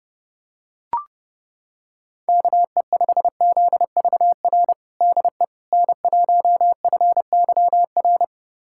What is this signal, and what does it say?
Morse code, heard as a keyed tone, about 32 wpm, 700 Hz. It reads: KE5ZVR DE N1FYR
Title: USA Callsign DE Callsign 30wpm